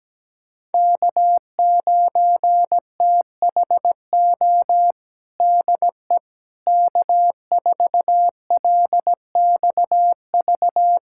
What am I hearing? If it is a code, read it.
Morse code, heard as a keyed tone, about 17 wpm, 700 Hz. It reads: K9THO DE K4LXV
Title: USA Callsign DE Callsign 17wpm